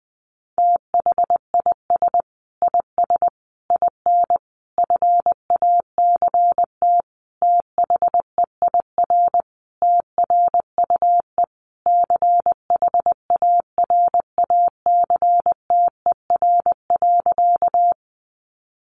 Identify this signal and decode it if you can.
Morse code, heard as a keyed tone, about 20 wpm, 700 Hz. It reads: THIS IS IN FACT THEIR TRUE CHARACTER.
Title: Sentences from Top 1200 Words 20wpm